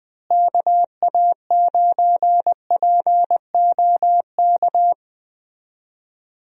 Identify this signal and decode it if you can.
Morse code, heard as a keyed tone, about 20 wpm, 700 Hz. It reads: KA9POK
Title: USA Call Signs - 6 Characters - Repeated 3x - 20wpm